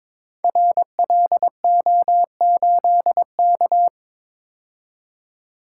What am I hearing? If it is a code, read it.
Morse code, heard as a keyed tone, about 22 wpm, 690 Hz. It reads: RLO8K
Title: Code Groups - 5 Letter:Numbers - 22wpm